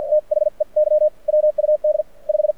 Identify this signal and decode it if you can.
Morse code, heard as a keyed tone, about 53 wpm, spaced as for 43 wpm, 590 Hz. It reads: THEY WUD 5
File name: archived / CRSnet